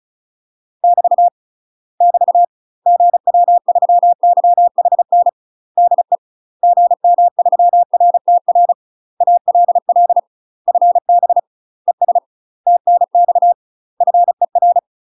Morse code, heard as a keyed tone, about 36 wpm, 700 Hz. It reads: <BT> <BT> GW3YHN DE GM3RTR ALL FB ES TNX FER
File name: UK MASTER QSOs TRAINING / 35WPM / UK MASTER QSOs 0007